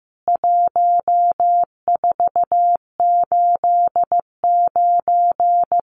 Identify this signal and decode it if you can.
Morse code, heard as a keyed tone, about 15 wpm, 700 Hz. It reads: 1489
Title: CWT Exchange 15wpm